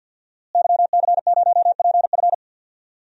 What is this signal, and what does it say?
Morse code, heard as a keyed tone, about 50 wpm, 700 Hz. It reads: YC0PF
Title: DXCC-100 Call Signs - 5 Characters - Repeated 2x - 50wpm